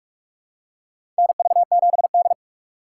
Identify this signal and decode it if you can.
Morse code, heard as a keyed tone, about 46 wpm, 700 Hz. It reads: NV7D
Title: USA Call Signs - 4 Characters - 3x Character Spacing - 45wpm